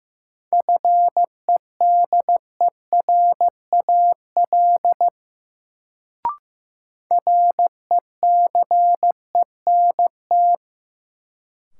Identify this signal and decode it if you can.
Morse code, heard as a keyed tone, about 15 wpm, 700 Hz. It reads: FEDERAL RECENT
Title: Sets of 3 Words - New Word Drill - 15wpm